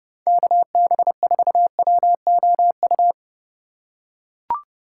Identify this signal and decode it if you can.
Morse code, heard as a keyed tone, about 29 wpm, 700 Hz. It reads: KB4WOU
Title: USA Call Signs - 6 Characters - 3x Character Spacing - 30wpm